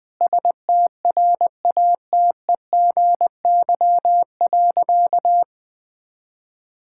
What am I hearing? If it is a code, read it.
Morse code, heard as a keyed tone, about 19 wpm, 700 Hz. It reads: STRATEGY.
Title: Sentences from Top 1000 Words - Review Part 5 - 20wpm